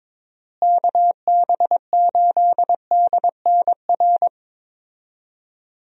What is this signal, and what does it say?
Morse code, heard as a keyed tone, about 22 wpm, 700 Hz. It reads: KB8DNR
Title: USA Call Signs - 6 Characters - Repeated 2x - 22wpm